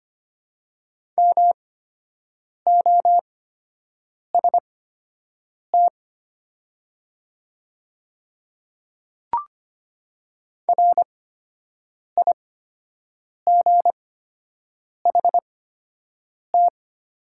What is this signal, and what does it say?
Morse code, heard as a keyed tone, about 25 wpm, spaced as for 7 wpm, 700 Hz. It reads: MOST RIGHT
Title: 8x Character Spacing - Top 200 Words - 25wpm